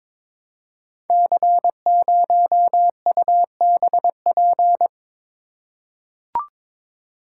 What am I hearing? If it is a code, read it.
Morse code, heard as a keyed tone, about 22 wpm, 700 Hz. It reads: C0UBP
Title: Code Groups - 5 Letter:Numbers - 22wpm